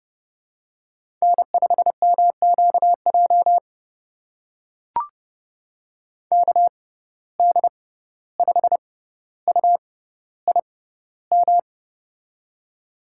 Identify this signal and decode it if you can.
Morse code, heard as a keyed tone, about 30 wpm, spaced as for 15 wpm, 700 Hz. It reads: N5MQJ KD5UIM
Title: USA Call Signs - 6x Character Spacing - 30wpm